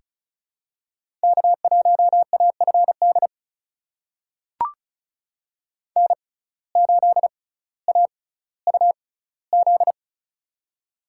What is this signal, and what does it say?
Morse code, heard as a keyed tone, about 36 wpm, spaced as for 18 wpm, 700 Hz. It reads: K1AFD N8AUZ
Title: USA Call Signs - 5 Characters - 6x Character Spacing - 35wpm